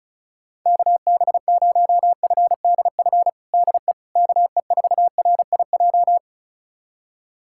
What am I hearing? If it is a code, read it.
Morse code, heard as a keyed tone, about 36 wpm, 700 Hz. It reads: KB0FDF DE KE4RIJ
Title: USA Callsign DE Callsign 35wpm